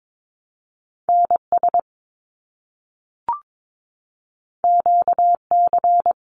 Morse code, heard as a keyed tone, about 22 wpm, 700 Hz. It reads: NS QC
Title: Canadian Provinces and Territories - Repeated 3x - 22wpm